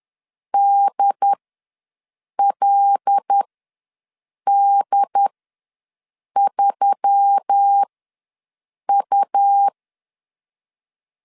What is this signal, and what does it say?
Morse code, heard as a keyed tone, about 10 wpm, spaced as for 6 wpm, 800 Hz. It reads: DLD3U
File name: LEZIONI / 5 LBVC / 5 L ws 10wpm 3636